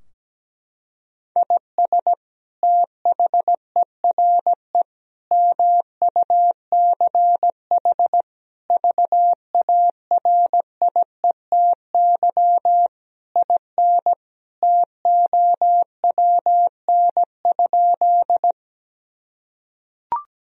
Morse code, heard as a keyed tone, about 17 wpm, 700 Hz. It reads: IS THERE MUCH VARIETY IN TOWN?
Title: Sentences from Top 1000 Words - Review Part 4 - 17wpm